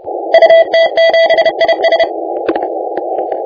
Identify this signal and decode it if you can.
Morse code, heard as a keyed tone, about 33 wpm, 660 Hz, with noise. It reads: UT7IS
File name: la4c / sac-cw-2018 / CONTEST AUDIO 1600Z